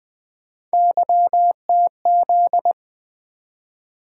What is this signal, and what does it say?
Morse code, heard as a keyed tone, about 20 wpm, 700 Hz. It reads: YTZ
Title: Three Letter-Number Combinations - 20wpm